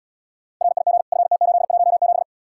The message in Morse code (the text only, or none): RIG FT817